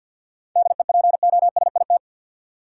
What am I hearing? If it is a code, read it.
Morse code, heard as a keyed tone, about 52 wpm, 700 Hz. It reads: DEPOSIT
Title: Sentences from Top 1300 Words - New Word Drill - 50wpm